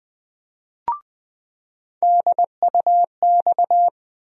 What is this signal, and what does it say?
Morse code, heard as a keyed tone, about 21 wpm, 700 Hz. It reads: DUX
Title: USA Call Sign Postfixes - 3 Letters 20wpm